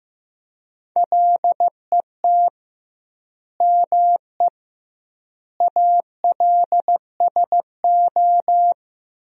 Morse code, heard as a keyed tone, about 16 wpm, 700 Hz. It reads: LET ME ALSO
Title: 2x Word-Spacing - Sentences from Top 400 Words - 15wpm